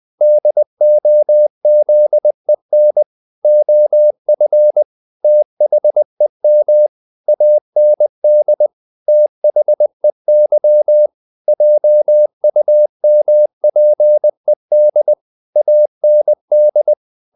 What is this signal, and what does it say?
Morse code, heard as a keyed tone, about 20 wpm, 600 Hz. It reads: DOZEN OF THEM AND THEY JUMPED AND
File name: the old man and the sea part 2 20wpm